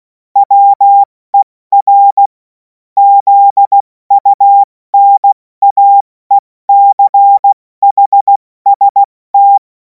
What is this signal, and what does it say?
Morse code, heard as a keyed tone, about 16 wpm, 800 Hz. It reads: WER ZUNAECHST
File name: Morsekurs / Witze / Witz 24. Februar 2026 70 F80